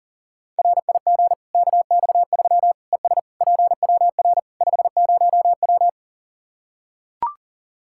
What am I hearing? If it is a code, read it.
Morse code, heard as a keyed tone, about 37 wpm, 700 Hz. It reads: RIG KX3 ES PWR 50W
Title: Practice QSO - Exchange 2 - Info - 40wpm